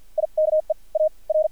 Morse code, heard as a keyed tone, about 55 wpm, spaced as for 43 wpm, 620 Hz. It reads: IKE A R
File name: archived / CRSnet